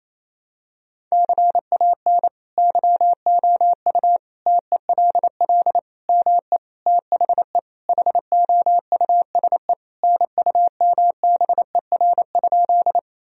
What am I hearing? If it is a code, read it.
Morse code, heard as a keyed tone, about 28 wpm, 700 Hz. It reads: CAN YOU TELL ME THE HOUSE NUMBER?
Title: Sentences from Top 1000 Words - Review Part 1 - 28wpm